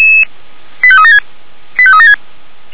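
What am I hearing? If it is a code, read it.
Five-tone selective call (ZVEI), heard as quick runs of five tones, tones of about 70 ms. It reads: 85376 85376
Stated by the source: Five-tone ZVEI selective call received on an SDR.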